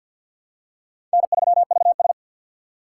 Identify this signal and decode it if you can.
Morse code, heard as a keyed tone, about 53 wpm, 690 Hz. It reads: N3VS
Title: USA Call Signs - 4 Characters - 6x Character Spacing - 50wpm